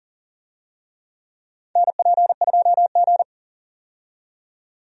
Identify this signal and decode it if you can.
Morse code, heard as a keyed tone, about 40 wpm, 700 Hz. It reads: NP2G
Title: USA Call Signs - 4 Characters - 40wpm